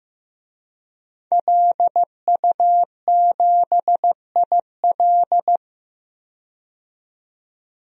Morse code, heard as a keyed tone, about 15 wpm, 700 Hz. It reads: LU7IL
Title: DXCC-100 Call Signs - 5 Characters - 15wpm